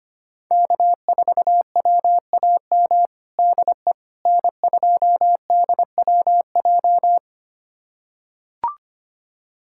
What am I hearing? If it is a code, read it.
Morse code, heard as a keyed tone, about 25 wpm, 700 Hz. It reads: K4WAM DE N2DWJ
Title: USA Callsign DE Callsign - 5 Characters - 25wpm